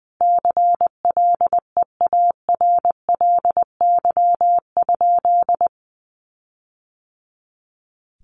Morse code, heard as a keyed tone, about 20 wpm, 700 Hz. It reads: CLEARLY?
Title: Sentences from Top 1300 Words 20wpm